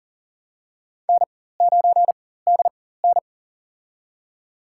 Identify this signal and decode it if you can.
Morse code, heard as a keyed tone, about 40 wpm, spaced as for 19 wpm, 700 Hz. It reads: N9DN
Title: USA Call Signs - 4 Characters - 4x Character Spacing - 40wpm